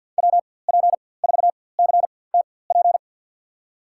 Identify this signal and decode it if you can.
Morse code, heard as a keyed tone, about 51 wpm, spaced as for 23 wpm, 700 Hz. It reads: WP4CTP
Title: USA Call Signs - 6 Characters - 4x Character Spacing - 50wpm